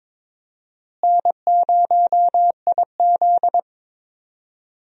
Morse code, heard as a keyed tone, about 22 wpm, 700 Hz. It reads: N0IZ
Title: USA Call Signs - 4 Characters - Repeated 3x - 22wpm